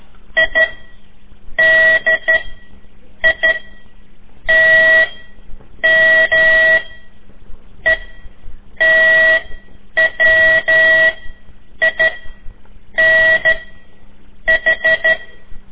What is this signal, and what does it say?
Morse code, heard as a keyed tone, about 12 wpm, spaced as for 7 wpm, 1950 Hz. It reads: IDITMETWINH
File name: ik1weg2 / CORSI CW / lez4-3